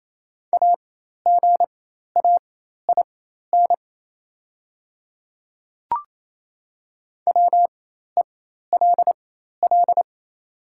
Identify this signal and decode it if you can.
Morse code, heard as a keyed tone, about 27 wpm, spaced as for 13 wpm, 700 Hz. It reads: AGAIN WELL
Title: 4x Character Spacing - Top 200 Words - 28wpm